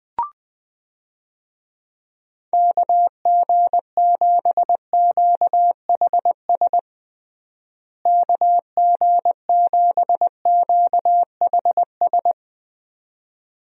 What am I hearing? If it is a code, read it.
Morse code, heard as a keyed tone, about 20 wpm, 700 Hz. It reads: KG7QHS KG7QHS
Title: USA Call Signs - 6 Characters - Repeated 2x - 20wpm